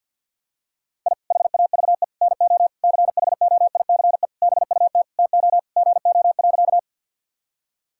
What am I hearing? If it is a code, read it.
Morse code, heard as a keyed tone, about 52 wpm, 700 Hz. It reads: I HAVE NO CHOICE BUT TO GO.
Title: Sentences from Top 1000 Words - Review Part 6 - 50wpm